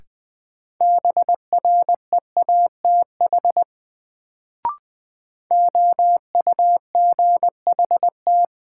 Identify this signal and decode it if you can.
Morse code, heard as a keyed tone, about 20 wpm, 700 Hz. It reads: BREATH OUGHT
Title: Sets of 5 Words Encore - New Word Drill - 20wpm